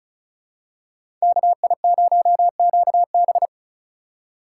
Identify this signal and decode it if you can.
Morse code, heard as a keyed tone, about 34 wpm, 700 Hz. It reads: KI0QB
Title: USA Call Signs - 5 Characters - 6x Character Spacing - 35wpm